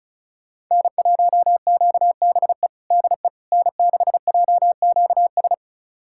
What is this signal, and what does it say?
Morse code, heard as a keyed tone, about 34 wpm, 700 Hz. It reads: N1QBE DE N6JQS
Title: USA Callsign DE Callsign - 5 Characters - 35wpm